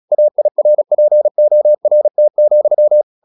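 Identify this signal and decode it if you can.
Morse code, heard as a keyed tone, about 36 wpm, 600 Hz. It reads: AIRPORT,
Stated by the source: Morse practice at 36 wpm, text of Ken Follett's novel Never, from CW med Gnister.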